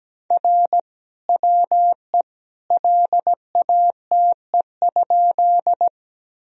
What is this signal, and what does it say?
Morse code, heard as a keyed tone, about 17 wpm, 700 Hz. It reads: R WE LATE?
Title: Sentences from Top 300 Words and Abbreviations 17wpm